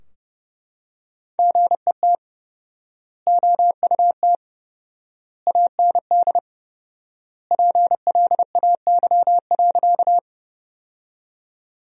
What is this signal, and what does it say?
Morse code, heard as a keyed tone, about 31 wpm, 700 Hz. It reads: GET OUT AND PLAY.
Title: 4x Word-Spacing - Sentences from Top 200 Words - 30wpm